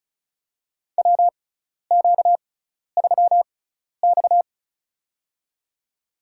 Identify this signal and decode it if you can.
Morse code, heard as a keyed tone, about 36 wpm, spaced as for 12 wpm, 700 Hz. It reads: WQ3X
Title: USA Call Signs - 4 Characters - 6x Character Spacing - 35wpm